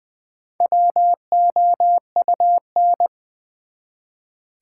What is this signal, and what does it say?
Morse code, heard as a keyed tone, about 20 wpm, 700 Hz. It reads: WOUN
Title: Build A Word - 1 Letter At A Time - 3 to 5-Letter Words - 20wpm